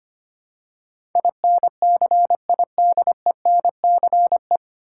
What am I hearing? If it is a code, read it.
Morse code, heard as a keyed tone, about 25 wpm, 700 Hz. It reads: INCIDENCE
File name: Nine Letter Words - 25wpm